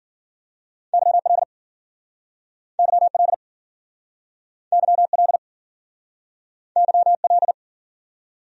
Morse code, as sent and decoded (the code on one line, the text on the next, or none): -.-- .-.. / -.-- .-.. / -.-- .-.. / -.-- .-..
YL YL YL YL